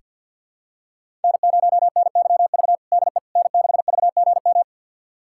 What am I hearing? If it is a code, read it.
Morse code, heard as a keyed tone, about 46 wpm, 700 Hz. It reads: N0NYV DE N6VGM
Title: USA Callsign DE Callsign - 5 Characters - 50wpm